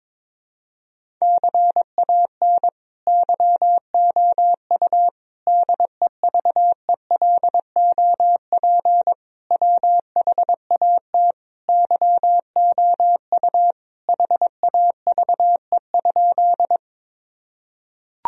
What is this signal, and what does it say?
Morse code, heard as a keyed tone, about 21 wpm, 700 Hz. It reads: CAN YOU DEVELOP WHAT YOU HAVE?
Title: Sentences from Top 1000 Words - Review Part 2 - 22wpm